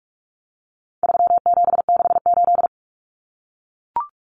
Morse code, heard as a keyed tone, about 44 wpm, 700 Hz. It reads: 3768
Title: Numbers - 4-Digit 45wpm